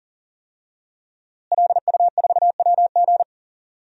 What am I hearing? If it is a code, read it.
Morse code, heard as a keyed tone, about 40 wpm, 700 Hz. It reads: LU4WG
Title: DXCC-100 Call Signs - 5 Characters - Repeated 3x - 40wpm